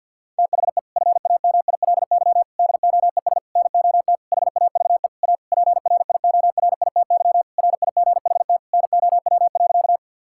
Morse code, heard as a keyed tone, about 50 wpm, 700 Hz. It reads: THE FAMILY DOES NOT HAVE A PRIORITY RIGHT NOW.
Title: Sentences from Top 1600 Words 50wpm